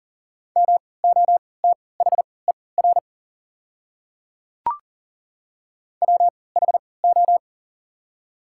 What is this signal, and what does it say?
Morse code, heard as a keyed tone, about 38 wpm, spaced as for 22 wpm, 690 Hz. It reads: MOTHER WHO
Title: 3x Character Spacing - Top 200 Words - 40wpm